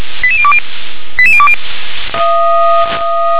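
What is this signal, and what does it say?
Five-tone selective call (ZVEI), heard as quick runs of five tones, tones of about 70 ms. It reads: 88020 88020
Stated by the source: ZVEI five-tone selective call received with an SDR.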